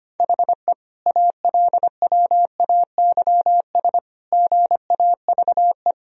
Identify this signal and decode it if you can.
Morse code, heard as a keyed tone, about 25 wpm, 700 Hz. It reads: HE ALWAYS GAVE
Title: Sentences from Top 1000 Words - Review Part 3 - 25wpm